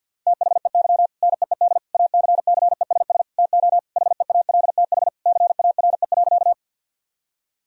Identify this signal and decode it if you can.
Morse code, heard as a keyed tone, about 50 wpm, 700 Hz. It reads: THEY NEED ACCESS TO HEALTH CARE.
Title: Sentences from Top 1000 Words - Review Part 4 - 50wpm